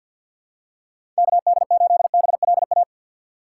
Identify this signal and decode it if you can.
Morse code, heard as a keyed tone, about 53 wpm, 700 Hz. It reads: KD8BLA
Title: USA Call Signs - 6 Characters - 4x Character Spacing - 50wpm